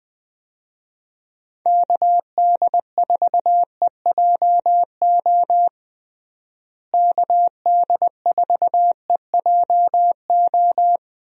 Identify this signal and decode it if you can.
Morse code, heard as a keyed tone, about 20 wpm, 700 Hz. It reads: KD4EJO KD4EJO
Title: USA Call Signs - 6 Characters - Repeated 3x - 20wpm